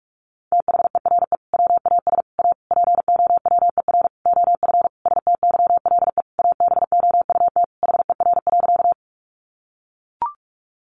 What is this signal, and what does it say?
Morse code, heard as a keyed tone, about 44 wpm, 700 Hz. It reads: THERE WAS A POWER OF STYLE ABOUT HER.
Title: Sentences from Top 800 Words 45wpm